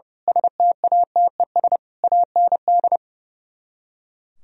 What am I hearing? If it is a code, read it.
Morse code, heard as a keyed tone, about 30 wpm, 700 Hz. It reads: STATES AND
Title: Speed-Racing - Sets of 5 Words 30wpm